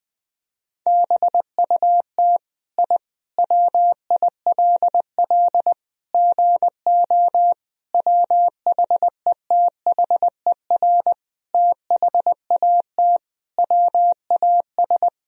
Morse code, heard as a keyed tone, about 20 wpm, 700 Hz. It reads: BUT I WILL GO WHETHER THAT WAS
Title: Sentences from Top 1000 Words - Review Part 3 - 20wpm